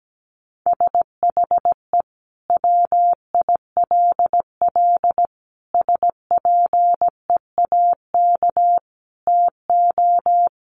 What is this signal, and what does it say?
Morse code, heard as a keyed tone, about 17 wpm, 700 Hz. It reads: SHE WILL SPEAK TO
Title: Sentences from Top 1100 Words 17wpm